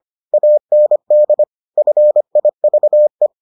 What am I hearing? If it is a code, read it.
Morse code, heard as a keyed tone, about 25 wpm, 600 Hz. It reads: AND FIVE